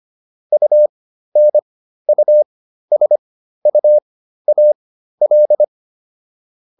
Morse code, heard as a keyed tone, about 25 wpm, spaced as for 13 wpm, 600 Hz. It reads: UNUSUAL